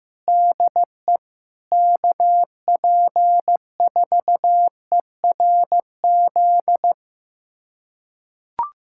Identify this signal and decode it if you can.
Morse code, heard as a keyed tone, about 15 wpm, 700 Hz. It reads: DE KP4ERZ
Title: USA Callsign DE Callsign - 6 Characters - 15wpm